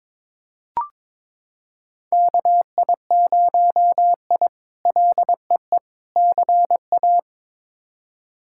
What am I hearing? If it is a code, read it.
Morse code, heard as a keyed tone, about 21 wpm, 700 Hz. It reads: KI0I LEE CA
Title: K1USN SST QSO Practice - 22wpm